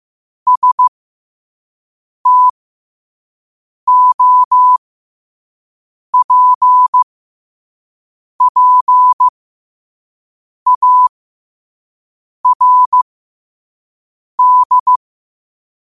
Morse code, heard as a keyed tone, about 15 wpm, spaced as for 5 wpm, 1000 Hz. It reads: STOPPARD
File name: qotdfiles / 5wpm